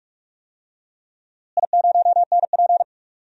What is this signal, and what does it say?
Morse code, heard as a keyed tone, about 43 wpm, 700 Hz. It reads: I0NP
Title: DXCC-100 Call Signs - 4 Characters - Repeated 3x - 45wpm